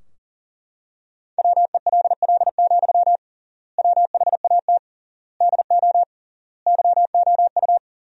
Morse code, heard as a keyed tone, about 40 wpm, 700 Hz. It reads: WELL, WHAT DO YOU
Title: 3x Word-Spacing - Sentences from Top 200 Words - 40wpm